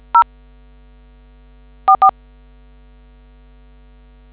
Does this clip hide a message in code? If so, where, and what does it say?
Yes: DTMF from 0.1 to 2.1 s, heard as telephone touch tones, tones of about 100 ms. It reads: 0 44